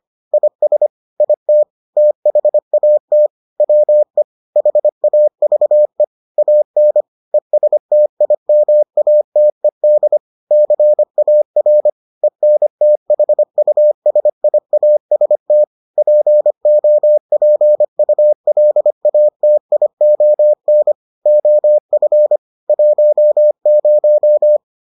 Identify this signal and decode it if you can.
Morse code, heard as a keyed tone, about 25 wpm, 600 Hz. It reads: IS IT THAT WE HAVE AN ESTIMATED CAR ENTHUSIAST POPULATION OF 10